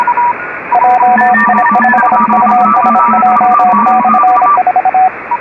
Morse code, heard as a keyed tone, about 26 wpm, 740 Hz, with radio noise. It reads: JI5USJ/4
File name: RS44 JA9KRO